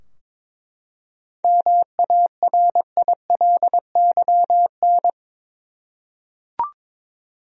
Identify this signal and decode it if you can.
Morse code, heard as a keyed tone, about 22 wpm, 700 Hz. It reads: MARILYN
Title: USA Common Names - 22wpm